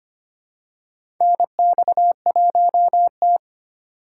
Morse code, heard as a keyed tone, about 25 wpm, 700 Hz. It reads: NX1T
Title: USA Call Signs - 4 Characters - 2x Character Spacing - 25wpm